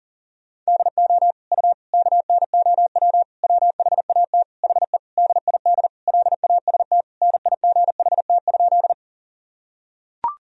Morse code, heard as a keyed tone, about 40 wpm, 700 Hz. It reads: DO U KNOW WHAT HE DID LAST NIGHT?
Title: Sentences from Top 300 Words and Abbreviations 40wpm